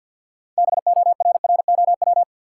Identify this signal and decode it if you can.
Morse code, heard as a keyed tone, about 53 wpm, 700 Hz. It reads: BORROW
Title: Sentences from Top 1500 Words - New Word Drill - 50wpm